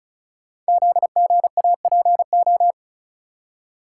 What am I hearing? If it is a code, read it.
Morse code, heard as a keyed tone, about 36 wpm, 700 Hz. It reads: ZGAPO
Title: Code Groups - 5 Letters - 35wpm